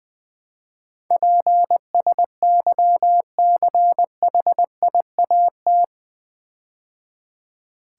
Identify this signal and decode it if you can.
Morse code, heard as a keyed tone, about 21 wpm, 700 Hz. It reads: PSYCHIAT
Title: Build A Word - 4 Letters At A Time - 11-Letter Words - 20wpm